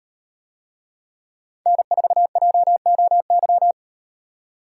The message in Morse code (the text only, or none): N4JOY